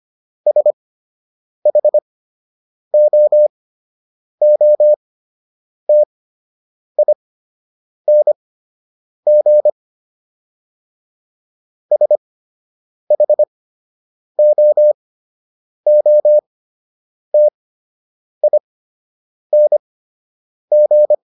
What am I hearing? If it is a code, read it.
Morse code, heard as a keyed tone, about 25 wpm, spaced as for 8 wpm, 600 Hz. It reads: SHOOTING SHOOTING